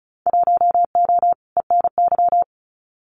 Morse code, heard as a keyed tone, about 34 wpm, 700 Hz. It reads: 1O ENY
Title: Winter Field Day Exchanges 35wpm